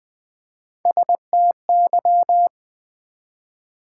Morse code, heard as a keyed tone, about 20 wpm, 690 Hz. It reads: STY
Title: Build A Word - 1 Letter At A Time - 5-Letter Words - 20wpm